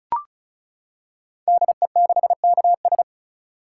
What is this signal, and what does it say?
Morse code, heard as a keyed tone, about 36 wpm, 700 Hz. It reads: DE6KS
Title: Code Groups - 5 Letter:Numbers - 35wpm